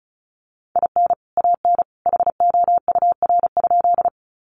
Morse code, heard as a keyed tone, about 34 wpm, 700 Hz. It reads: IN AN HOUR?
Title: Sentences from Top 400 Words 35wpm